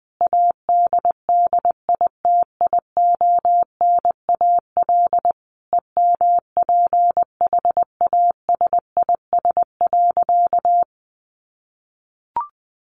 Morse code, heard as a keyed tone, about 19 wpm, 700 Hz. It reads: ADDITIONAL EMPHASIS.
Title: Sentences from Top 1400 Words 20wpm